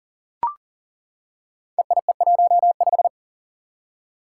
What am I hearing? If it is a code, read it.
Morse code, heard as a keyed tone, about 42 wpm, 700 Hz. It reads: EIE15
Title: Five Challenging Letter-Number Combinations 40wpm